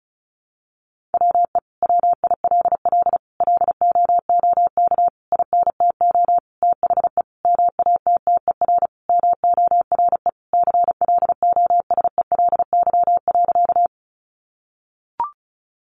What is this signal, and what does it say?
Morse code, heard as a keyed tone, about 34 wpm, 700 Hz. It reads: WE WILL LOOK INTO THE MATTER MORE CLOSELY.
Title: Sentences from Top 1400 Words 35wpm